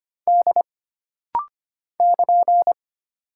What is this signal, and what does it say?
Morse code, heard as a keyed tone, about 25 wpm, 700 Hz. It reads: D <KN>
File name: Single Character (Letter:Number:Punctuation:Prosign) - Rapid-Fire - 25wpm